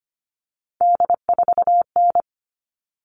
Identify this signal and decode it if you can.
Morse code, heard as a keyed tone, about 25 wpm, 700 Hz. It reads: D4N
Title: All Cut Numbers - 3-Digit 25wpm